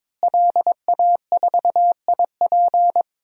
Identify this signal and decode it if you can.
Morse code, heard as a keyed tone, about 23 wpm, 700 Hz. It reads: LA4IP
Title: DXCC-100 Call Signs - 5 Characters - Repeated 2x - 22wpm